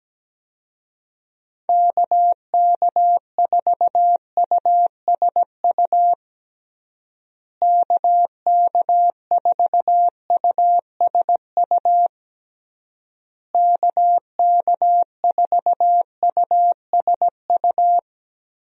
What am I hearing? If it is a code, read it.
Morse code, heard as a keyed tone, about 17 wpm, 700 Hz. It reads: KK4USU KK4USU KK4USU
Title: USA Call Signs - 6 Characters - Repeated 3x - 17wpm